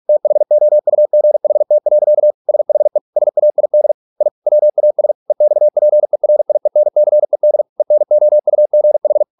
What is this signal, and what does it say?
Morse code, heard as a keyed tone, about 46 wpm, 600 Hz. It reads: THOUGHT. SHE SAID I WAS EXPERIENCED ENOUGH